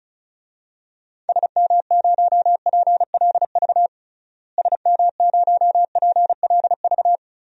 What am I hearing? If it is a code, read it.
Morse code, heard as a keyed tone, about 36 wpm, 700 Hz. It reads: SM0PLV SM0PLV
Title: DXCC-100 Call Signs - 6 Characters - Repeated 2x - 35wpm